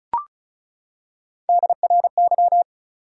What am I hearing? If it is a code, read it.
Morse code, heard as a keyed tone, about 36 wpm, 700 Hz. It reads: DRY
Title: Three Letter Words - 35wpm